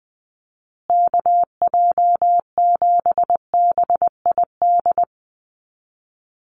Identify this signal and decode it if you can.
Morse code, heard as a keyed tone, about 20 wpm, 700 Hz. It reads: KJ7BID
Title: USA Call Signs Repeated 3x 20wpm